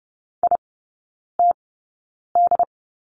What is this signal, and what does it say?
Morse code, heard as a keyed tone, about 30 wpm, spaced as for 9 wpm, 700 Hz. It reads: ITD
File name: 081 Lesson 14e - Number 2 - Three Characters 30wpm